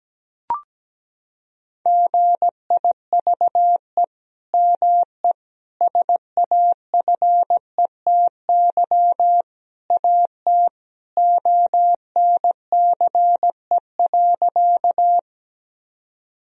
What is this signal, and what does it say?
Morse code, heard as a keyed tone, about 17 wpm, 700 Hz. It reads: GIVE ME SAFETY AT ONCE.
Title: Sentences from Top 1000 Words - Review Part 4 - 17wpm